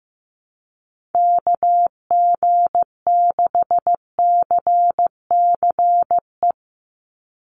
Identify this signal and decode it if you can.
Morse code, heard as a keyed tone, about 15 wpm, 700 Hz. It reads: KG6CCE
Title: USA Call Signs 15wpm